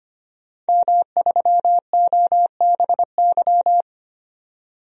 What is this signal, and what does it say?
Morse code, heard as a keyed tone, about 25 wpm, 700 Hz. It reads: M3OBY
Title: DXCC-100 Call Signs - 5 Characters - 25wpm